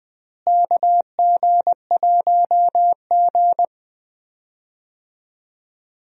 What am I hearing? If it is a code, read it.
Morse code, heard as a keyed tone, about 21 wpm, 700 Hz. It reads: KG1G
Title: USA Call Signs - 4 Characters - Repeated 2x - 20wpm